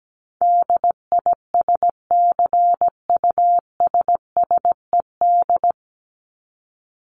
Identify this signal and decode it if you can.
Morse code, heard as a keyed tone, about 17 wpm, 700 Hz. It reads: DISCUSSED
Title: ED Suffix - Top 100 Words with Base Words 17wpm